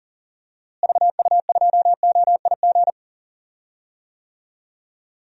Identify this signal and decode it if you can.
Morse code, heard as a keyed tone, about 40 wpm, 700 Hz. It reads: VU2OIG
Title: DXCC-100 Call Signs - 6 Characters - Repeated 2x - 40wpm